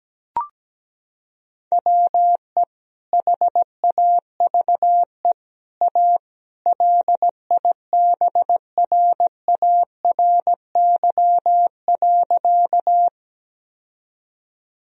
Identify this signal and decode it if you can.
Morse code, heard as a keyed tone, about 17 wpm, 700 Hz. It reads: WE HAVE A LIBRARY.
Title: Sentences from Top 1000 Words - Review Part 4 - 17wpm